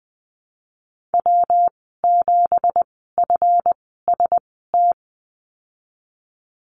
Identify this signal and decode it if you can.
Morse code, heard as a keyed tone, about 20 wpm, spaced as for 14 wpm, 700 Hz. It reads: W7FST
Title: 173 Lesson 28g - Number 9 - Call Signs 20wpm